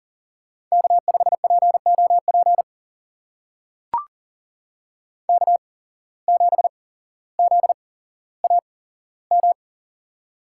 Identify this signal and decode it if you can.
Morse code, heard as a keyed tone, about 39 wpm, spaced as for 17 wpm, 700 Hz. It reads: K5POP K7ZAM
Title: USA Call Signs - 5 Characters - 8x Character Spacing - 40wpm